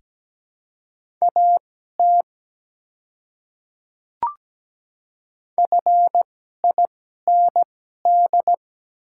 Morse code, heard as a keyed tone, about 17 wpm, spaced as for 12 wpm, 700 Hz. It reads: AT FIND
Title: 2x Character Spacing - Top 100 Words - 17wpm